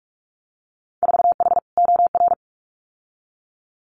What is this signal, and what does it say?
Morse code, heard as a keyed tone, about 46 wpm, 700 Hz. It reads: 4H OR
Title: Winter Field Day Exchanges Encore 45wpm